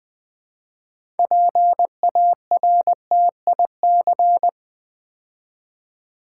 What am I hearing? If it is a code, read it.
Morse code, heard as a keyed tone, about 20 wpm, 700 Hz. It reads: PARTIC
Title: Build A Word - 6 Letters At A Time - 12-Letter Words - 20wpm